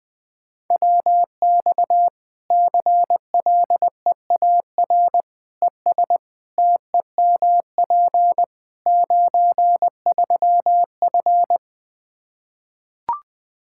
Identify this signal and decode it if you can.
Morse code, heard as a keyed tone, about 20 wpm, 700 Hz. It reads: WX CLEAR ES TEMP 93F
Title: K1USN SST QSO Practice 20wpm